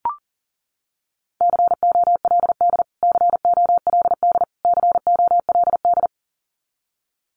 Morse code, heard as a keyed tone, about 37 wpm, 700 Hz. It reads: COLD COLD COLD
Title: Top 500 Words Repeated 40wpm